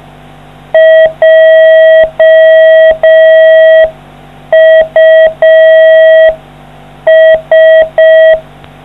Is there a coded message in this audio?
Yes: Morse code from 0.7 to 8.3 s, heard as a keyed tone, about 5 wpm, 640 Hz. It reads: JUS